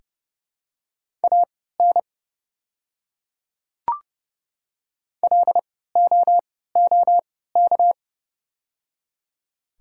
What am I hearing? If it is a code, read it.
Morse code, heard as a keyed tone, about 30 wpm, spaced as for 17 wpm, 700 Hz. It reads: AN LOOK
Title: 3x Character Spacing - Top 100 Words - 30wpm